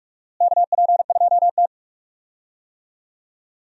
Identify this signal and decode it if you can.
Morse code, heard as a keyed tone, about 46 wpm, 700 Hz. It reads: KP2T
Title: USA Call Signs - 4 Characters - Repeated 3x - 45wpm